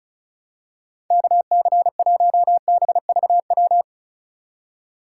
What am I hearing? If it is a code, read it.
Morse code, heard as a keyed tone, about 36 wpm, 700 Hz. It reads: KC1BVW
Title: USA Call Signs - 6 Characters - 2x Character Spacing - 35wpm